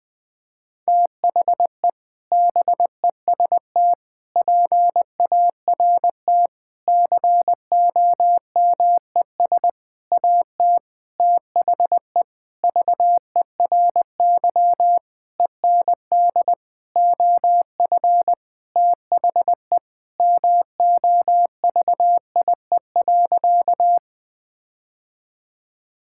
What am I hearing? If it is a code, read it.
Morse code, heard as a keyed tone, about 20 wpm, 700 Hz. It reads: THE BEST PART COMES AT THE VERY END OF THE MOVIE.
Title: Sentences with 6-Word Phrases - Spoken Beginning - 20wpm